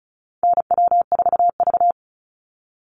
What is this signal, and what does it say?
Morse code, heard as a keyed tone, about 34 wpm, 700 Hz. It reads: NW4V
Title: Call Signs Repeated 3x - 35wpm